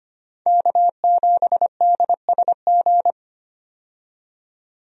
Morse code, heard as a keyed tone, about 25 wpm, 700 Hz. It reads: K7DSG
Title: USA Call Signs - 5 Characters - 25wpm